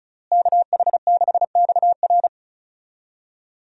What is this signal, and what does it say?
Morse code, heard as a keyed tone, about 34 wpm, 700 Hz. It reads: KH6XR
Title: DXCC-100 Call Signs - 5 Characters - Repeated 3x - 35wpm